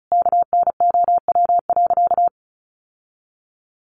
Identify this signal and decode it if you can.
Morse code, heard as a keyed tone, about 34 wpm, 700 Hz. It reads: KNOW.
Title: Sentences from Top 600 Words 35wpm